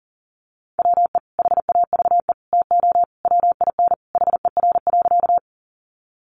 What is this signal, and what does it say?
Morse code, heard as a keyed tone, about 40 wpm, 700 Hz. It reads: WE HAVE TO WIN HER.
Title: Sentences from Top 1500 Words 40wpm